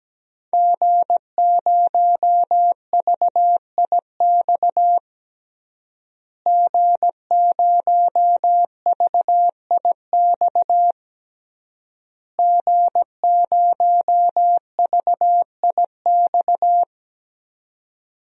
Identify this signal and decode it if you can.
Morse code, heard as a keyed tone, about 17 wpm, 700 Hz. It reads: G0VIX G0VIX G0VIX
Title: DXCC-100 Call Signs - 5 Characters - Repeated 3x - 17wpm